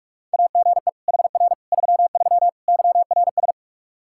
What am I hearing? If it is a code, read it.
Morse code, heard as a keyed tone, about 44 wpm, 700 Hz. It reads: AGE HR 33 YRS
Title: Practice QSO - Exchange 3 - More Info - 45wpm